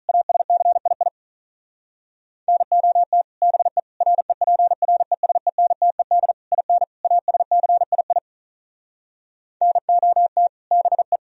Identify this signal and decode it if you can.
Morse code, heard as a keyed tone, about 43 wpm, 700 Hz. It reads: ASCII NOT BE REPRESENTED IN ASCII NOT BE
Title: Speed-Racing - Sets of 5 Words 35wpm